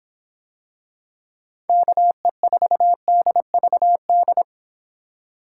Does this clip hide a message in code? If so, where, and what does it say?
Yes: Morse code from 1.7 to 4.4 s, heard as a keyed tone, about 26 wpm, 700 Hz. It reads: KE4DVD